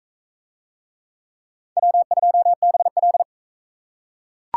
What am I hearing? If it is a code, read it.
Morse code, heard as a keyed tone, about 38 wpm, 700 Hz. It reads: W2BL